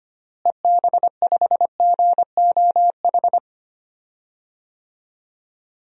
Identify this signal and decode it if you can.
Morse code, heard as a keyed tone, about 25 wpm, 700 Hz. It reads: EB5GOH
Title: DXCC-100 Call Signs - 6 Characters - Repeated 2x - 25wpm